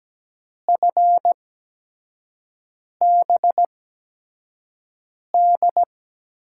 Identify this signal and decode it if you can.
Morse code, heard as a keyed tone, about 17 wpm, spaced as for 5 wpm, 700 Hz. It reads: FBD
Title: USA Call Signs - 5 Characters - 8x Character Spacing - 17wpm